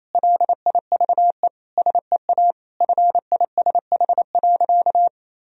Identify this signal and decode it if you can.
Morse code, heard as a keyed tone, about 28 wpm, 700 Hz. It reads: LIVE SEA FISH.
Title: Sentences from Top 1000 Words - Review Part 1 - 28wpm